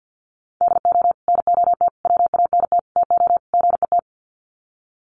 Morse code, heard as a keyed tone, about 48 wpm, 700 Hz. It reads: DO NOT WANT TO GET
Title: Sets of 5 Words 50wpm